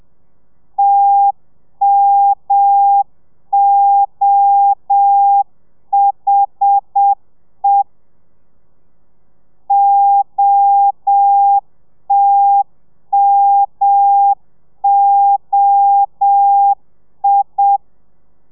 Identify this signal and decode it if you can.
Morse code, heard as a keyed tone, about 7 wpm, 800 Hz. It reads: TMOHE OTMOI
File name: telegrafia / i2tfj / Lez1 30